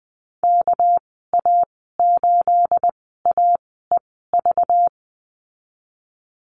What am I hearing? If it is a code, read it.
Morse code, heard as a keyed tone, about 20 wpm, spaced as for 14 wpm, 700 Hz. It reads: KA8AEV
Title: 245 Lesson 39g - Letter z - Call Signs 20wpm